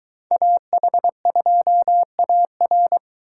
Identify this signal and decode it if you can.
Morse code, heard as a keyed tone, about 23 wpm, 700 Hz. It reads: AH2AR